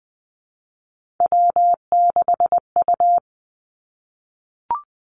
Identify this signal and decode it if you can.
Morse code, heard as a keyed tone, about 20 wpm, 700 Hz. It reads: W6U
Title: Three Challenging Letter-Number Combinations 20wpm